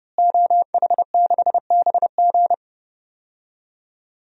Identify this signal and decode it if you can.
Morse code, heard as a keyed tone, about 30 wpm, 700 Hz. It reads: OH6BG
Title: DXCC-100 Call Signs - 5 Characters - Repeated 2x - 30wpm